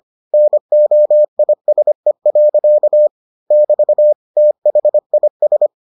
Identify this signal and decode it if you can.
Morse code, heard as a keyed tone, about 25 wpm, 600 Hz. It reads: NOISE. <BT> THIS